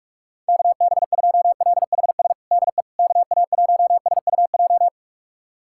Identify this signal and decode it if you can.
Morse code, heard as a keyed tone, about 46 wpm, 700 Hz. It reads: KB2LHS DE KA1SUJ
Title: USA Callsign DE Callsign 45wpm